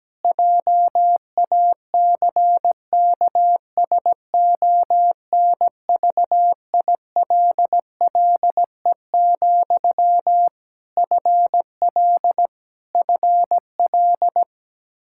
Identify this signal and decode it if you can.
Morse code, heard as a keyed tone, about 17 wpm, 700 Hz. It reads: JACKSONVILLE, FL FL
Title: Practice QSO - Exchange 1 - Beginning - 17wpm